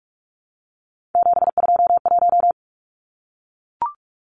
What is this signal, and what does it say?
Morse code, heard as a keyed tone, about 44 wpm, 700 Hz. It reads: 721
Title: Numbers - 3-Digit 45wpm